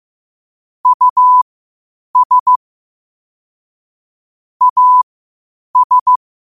Morse code, heard as a keyed tone, about 15 wpm, spaced as for 8 wpm, 1000 Hz. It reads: US AS
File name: qotdfiles / 8wpm